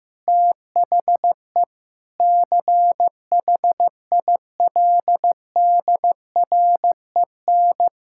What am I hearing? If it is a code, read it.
Morse code, heard as a keyed tone, about 15 wpm, 700 Hz. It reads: THE CHILDREN
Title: Sentences from Top 1000 Words - Review Part 6 - 15wpm